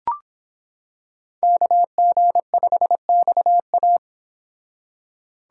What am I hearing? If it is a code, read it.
Morse code, heard as a keyed tone, about 26 wpm, 700 Hz. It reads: KG5XA